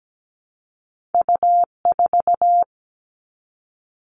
Morse code, heard as a keyed tone, about 17 wpm, 700 Hz. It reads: U4
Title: All Cut Numbers - 2-Digit 17wpm